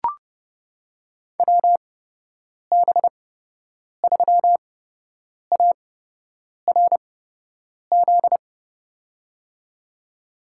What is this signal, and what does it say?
Morse code, heard as a keyed tone, about 30 wpm, spaced as for 8 wpm, 700 Hz. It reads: WB3ARZ
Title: USA Call Signs - 6 Characters - 8x Character Spacing - 30wpm